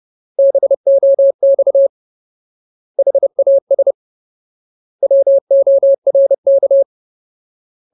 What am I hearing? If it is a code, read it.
Morse code, heard as a keyed tone, about 30 wpm, 550 Hz. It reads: BOX HAS WORK
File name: Ultimate Word List / 30WPM Ultimate